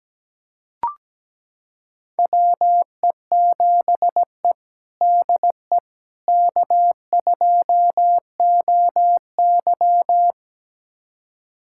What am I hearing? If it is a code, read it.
Morse code, heard as a keyed tone, about 17 wpm, 700 Hz. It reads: WE7E DE K2OY
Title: USA Callsign DE Callsign - 4 Characters - 17wpm